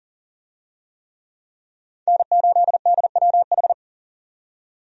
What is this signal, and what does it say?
Morse code, heard as a keyed tone, about 40 wpm, 700 Hz. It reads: N8DWH
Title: USA Call Signs - 5 Characters - 40wpm